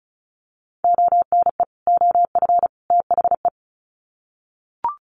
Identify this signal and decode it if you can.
Morse code, heard as a keyed tone, about 34 wpm, 700 Hz. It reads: ONE OF THE
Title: Sets of 3 Words 35wpm